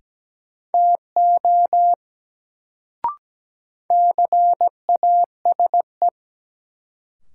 Morse code, heard as a keyed tone, about 17 wpm, 700 Hz. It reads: TO CASE
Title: Sets of 2 Words - New Word Drill - 17wpm